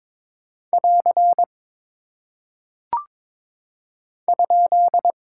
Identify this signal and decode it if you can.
Morse code, heard as a keyed tone, about 22 wpm, 700 Hz. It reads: <AR> ?
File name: practice / 22wpm / Single Punctuation-Prosign - 22wpm